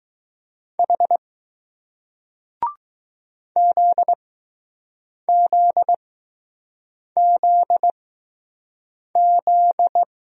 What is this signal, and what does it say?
Morse code, heard as a keyed tone, about 20 wpm, spaced as for 6 wpm, 700 Hz. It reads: H ZZZZ